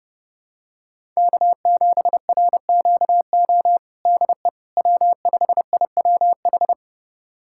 Morse code, heard as a keyed tone, about 32 wpm, 700 Hz. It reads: K7RQO DE W5IWH
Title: USA Callsign DE Callsign - 5 Characters - 30wpm